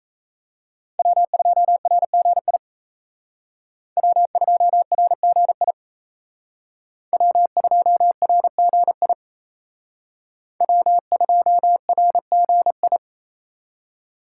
Morse code, heard as a keyed tone, about 33 wpm, 700 Hz. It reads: W2RGI W2RGI W2RGI W2RGI
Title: Speed Racing - USA 1x3 Call Signs 28wpm